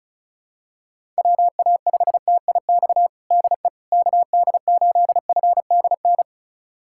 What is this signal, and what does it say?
Morse code, heard as a keyed tone, about 36 wpm, 700 Hz. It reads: WA5TIX DE KD8FDN
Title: USA Callsign DE Callsign - 6 Characters - 35wpm